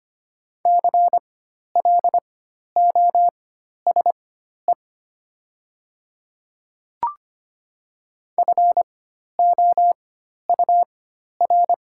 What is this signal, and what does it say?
Morse code, heard as a keyed tone, about 25 wpm, spaced as for 12 wpm, 700 Hz. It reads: CLOSE FOUR
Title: 4x Character Spacing - Top 300 Words - 25wpm